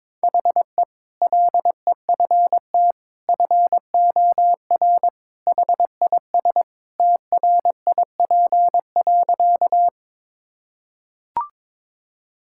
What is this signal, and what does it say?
Morse code, heard as a keyed tone, about 22 wpm, 700 Hz. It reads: HE LEFT FOR HIS TRIP.
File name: practice / 22wpm / Sentences with 2-Word Phrases - 22wpm